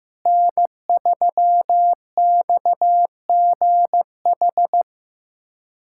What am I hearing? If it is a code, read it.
Morse code, heard as a keyed tone, about 15 wpm, 700 Hz. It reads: N3XGH
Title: USA Callsign DE Callsign 15wpm